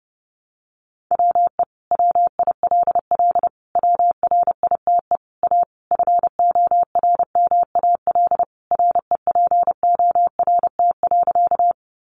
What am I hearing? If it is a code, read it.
Morse code, heard as a keyed tone, about 28 wpm, 700 Hz. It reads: WE WILL WRITE A FORMAL REPORT.
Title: Sentences from Top 1300 Words 30wpm